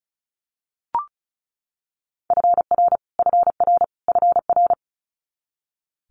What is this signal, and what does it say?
Morse code, heard as a keyed tone, about 36 wpm, 700 Hz. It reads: FR FR FR
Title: Multi-letter Phonemes Repeated 3x 35wpm